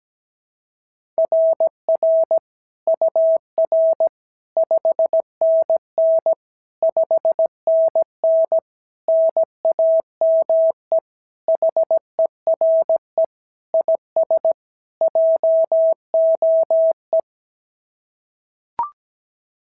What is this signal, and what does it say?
Morse code, heard as a keyed tone, about 16 wpm, 650 Hz. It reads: RR UR 5NN 5NN NAME HERE IS JOE
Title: SOTA QSO Practice 17wpm